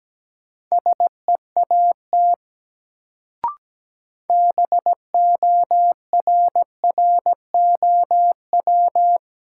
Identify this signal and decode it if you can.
Morse code, heard as a keyed tone, about 17 wpm, 700 Hz. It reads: SEAT BORROW
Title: Sentences from Top 1500 Words - New Word Drill Alt - 17wpm